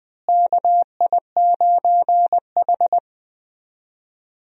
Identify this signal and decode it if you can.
Morse code, heard as a keyed tone, about 20 wpm, 700 Hz. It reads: KI9H
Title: USA Call Signs - 4 Characters - Repeated 2x - 20wpm